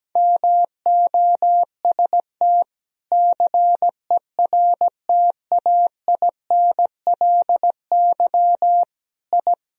Morse code, heard as a keyed tone, about 17 wpm, 700 Hz. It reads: MOST CERTAINLY I
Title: Sentences from Top 1000 Words - Review Part 1 - 17wpm